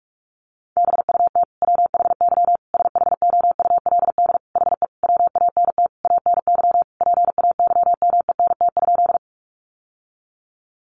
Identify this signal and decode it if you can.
Morse code, heard as a keyed tone, about 44 wpm, 700 Hz. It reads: BUT WHY SHOULD HE WANT ANY PAYMENT?
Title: Sentences from Top 1200 Words 45wpm